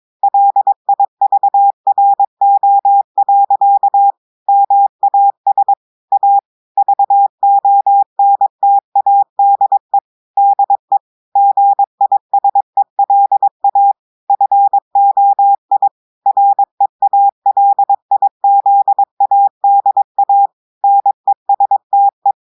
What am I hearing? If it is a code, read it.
Morse code, heard as a keyed tone, about 22 wpm, 800 Hz. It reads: LIVRO. MAS A VONTADE DE GISELA FOI REALIZADA NESTE
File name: audios / portugais / portugais texte en clair 1 vitesse 22 mots minutes